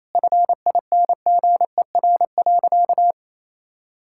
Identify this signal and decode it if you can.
Morse code, heard as a keyed tone, about 28 wpm, 700 Hz. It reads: FINGER.
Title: Sentences from Top 1600 Words 28wpm